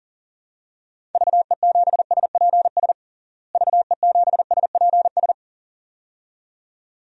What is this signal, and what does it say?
Morse code, heard as a keyed tone, about 37 wpm, 700 Hz. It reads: VE7SPS VE7SPS
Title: DXCC-100 Call Signs - 6 Characters - Repeated 2x - 40wpm